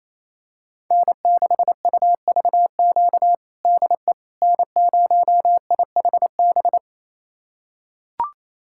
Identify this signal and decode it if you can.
Morse code, heard as a keyed tone, about 28 wpm, 700 Hz. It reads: N6UVQ DE N0IHB
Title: USA Callsign DE Callsign - 5 Characters - 28wpm